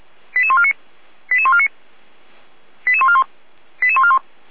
Five-tone selective call (ZVEI), heard as quick runs of five tones, tones of about 70 ms. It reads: 88149 88149 88141 88141